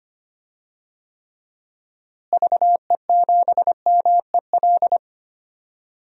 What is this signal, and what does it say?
Morse code, heard as a keyed tone, about 25 wpm, 700 Hz. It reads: VE7MEL
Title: DXCC-100 Call Signs - 6 Characters - Repeated 3x - 25wpm